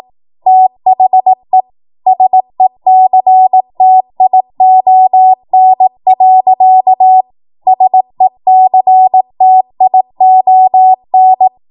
Morse code, heard as a keyed tone, about 18 wpm, 750 Hz. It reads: THE SECTION. SECTION
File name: Morse / 18WPM